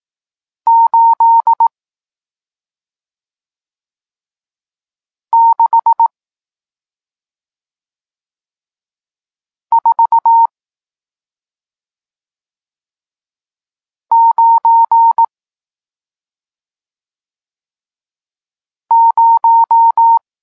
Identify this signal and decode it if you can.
Morse code, heard as a keyed tone, about 18 wpm, 920 Hz. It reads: 8 6 4 9 0
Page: Tekst 40 – Nauka znaków przestankowych